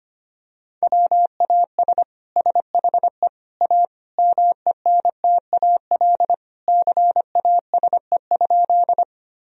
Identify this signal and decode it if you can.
Morse code, heard as a keyed tone, about 25 wpm, 700 Hz. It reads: WAS SHE A MENTAL CASE?
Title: Sentences from Top 1000 Words - Review Part 4 - 25wpm